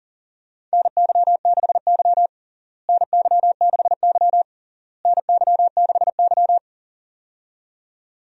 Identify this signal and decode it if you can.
Morse code, heard as a keyed tone, about 40 wpm, 700 Hz. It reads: NY6Y NY6Y NY6Y
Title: USA Call Signs - 4 Characters - Repeated 3x - 40wpm